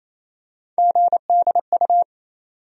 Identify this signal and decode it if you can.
Morse code, heard as a keyed tone, about 28 wpm, 700 Hz. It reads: GDU
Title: USA Call Sign Postfixes - 3 Letters 28wpm